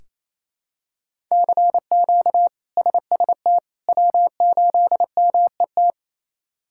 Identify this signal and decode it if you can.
Morse code, heard as a keyed tone, about 28 wpm, 700 Hz. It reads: CQ SST W8MET
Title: K1USN SST QSO Practice - 28wpm